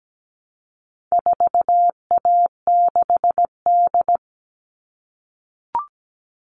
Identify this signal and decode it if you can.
Morse code, heard as a keyed tone, about 17 wpm, 700 Hz. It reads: 4A6D
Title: All Cut Numbers - 4-Digit 17wpm